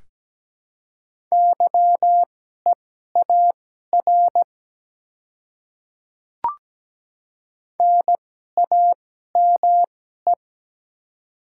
Morse code, heard as a keyed tone, about 17 wpm, spaced as for 12 wpm, 700 Hz. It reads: YEAR NAME
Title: 2x Character Spacing - Top 200 Words - 17wpm